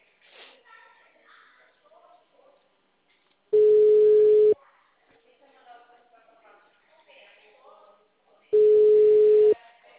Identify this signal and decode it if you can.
Ringback tone (Europe, 425 Hz), with speech.
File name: out- -2013- - - .37370